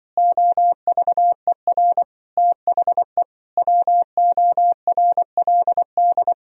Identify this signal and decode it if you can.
Morse code, heard as a keyed tone, about 25 wpm, 700 Hz. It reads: OVER THE WORLD